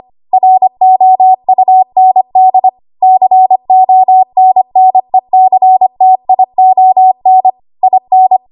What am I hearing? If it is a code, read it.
Morse code, heard as a keyed tone, about 25 wpm, 750 Hz. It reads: ROUND CONNECTION IN
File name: Archive / 25 WPM / 25WPM